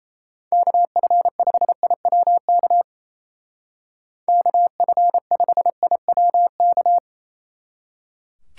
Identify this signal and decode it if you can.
Morse code, heard as a keyed tone, about 35 wpm, 700 Hz. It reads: KF5IWK KF5IWK
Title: Speed Racing - USA 2x3 Call Signs 28wpm